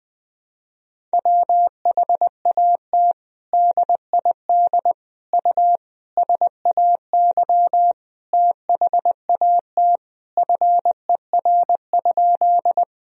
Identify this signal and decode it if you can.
Morse code, heard as a keyed tone, about 21 wpm, 700 Hz. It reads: WHAT DID U SAY THAT FER?
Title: Sentences from Top 200 Words and Abbreviations 20wpm